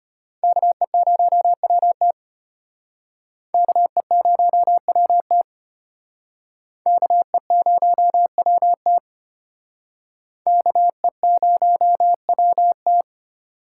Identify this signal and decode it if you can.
Morse code, heard as a keyed tone, about 32 wpm, 700 Hz. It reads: KE0WT KE0WT KE0WT KE0WT